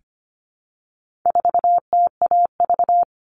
Morse code, heard as a keyed tone, about 25 wpm, 700 Hz. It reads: 4TAV
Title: All Cut Numbers - 4-Digit 25wpm